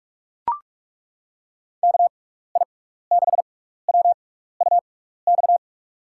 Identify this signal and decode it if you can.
Morse code, heard as a keyed tone, about 41 wpm, 700 Hz. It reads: K I 6 W U X
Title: USA Call Signs - 6 Characters - 6x Character Spacing - 45wpm